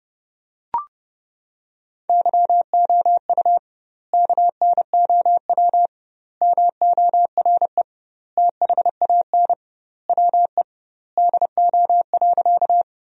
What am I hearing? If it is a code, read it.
Morse code, heard as a keyed tone, about 30 wpm, 700 Hz. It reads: YOU KNOW MORE THAN WE DO.
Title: 2x Word-Spacing - Sentences from Top 100 Words - 30wpm